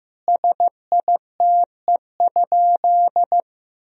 Morse code, heard as a keyed tone, about 15 wpm, 700 Hz. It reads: SITE?
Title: Sentences from Top 1000 Words - Review Part 4 - 15wpm